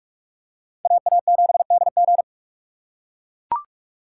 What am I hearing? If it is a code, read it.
Morse code, heard as a keyed tone, about 42 wpm, 700 Hz. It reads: AA7DG